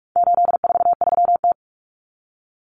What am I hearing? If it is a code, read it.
Morse code, heard as a keyed tone, about 44 wpm, 700 Hz. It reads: 843T
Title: Common Cut Numbers - 4-Digit 45wpm